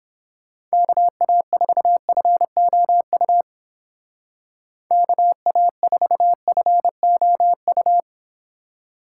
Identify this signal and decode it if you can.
Morse code, heard as a keyed tone, about 29 wpm, 700 Hz. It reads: KA4FOU KA4FOU